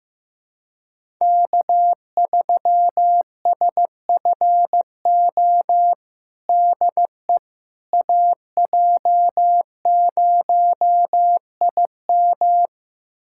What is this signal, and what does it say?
Morse code, heard as a keyed tone, about 15 wpm, 700 Hz. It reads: K3SFO DE AJ0IM
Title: USA Callsign DE Callsign - 5 Characters - 15wpm